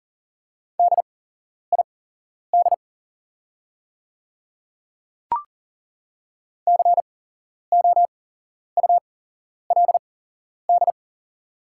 Morse code, heard as a keyed tone, about 40 wpm, spaced as for 11 wpm, 700 Hz. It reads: DID COULD
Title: 8x Character Spacing - Top 200 Words - 40wpm